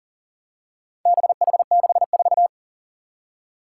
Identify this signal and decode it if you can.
Morse code, heard as a keyed tone, about 41 wpm, 700 Hz. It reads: BH64
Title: Four Challenging Letter-Number Combinations 40wpm